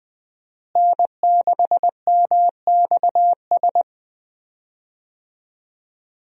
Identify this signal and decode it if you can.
Morse code, heard as a keyed tone, about 20 wpm, 700 Hz. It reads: N6MXS
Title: USA Call Signs - 5 Characters - Repeated 3x - 20wpm